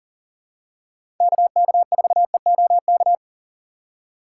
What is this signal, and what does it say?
Morse code, heard as a keyed tone, about 40 wpm, 700 Hz. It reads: KK4EOK
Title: USA Call Signs - 6x Character Spacing - 40wpm